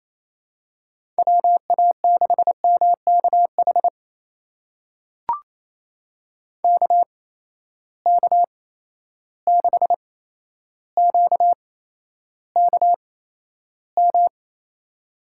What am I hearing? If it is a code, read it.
Morse code, heard as a keyed tone, about 27 wpm, spaced as for 12 wpm, 700 Hz. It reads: WA6MKH KK6QKM
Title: USA Call Signs - 6 Characters - 8x Character Spacing - 28wpm